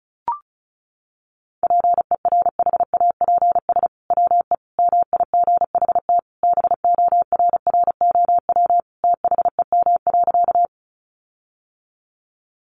Morse code, heard as a keyed tone, about 34 wpm, 700 Hz. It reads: PERHAPS WE MIGHT BORROW THEM.
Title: Sentences from Top 1500 Words 35wpm